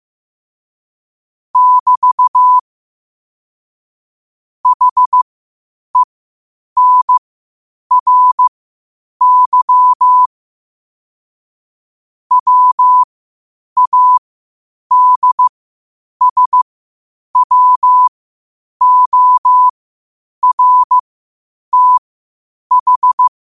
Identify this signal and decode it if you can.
Morse code, heard as a keyed tone, about 15 wpm, spaced as for 8 wpm, 1000 Hz. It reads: <BT> HENRY WADSWORTH
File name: qotdfiles / 8wpm